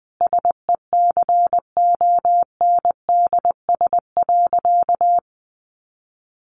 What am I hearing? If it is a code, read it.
Morse code, heard as a keyed tone, about 20 wpm, 700 Hz. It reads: SECONDS.
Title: A Sound Of Thunder 20wpm